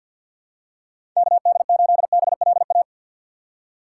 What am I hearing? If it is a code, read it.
Morse code, heard as a keyed tone, about 53 wpm, 700 Hz. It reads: KD8BLA
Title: USA Call Signs - 6 Characters - 6x Character Spacing - 50wpm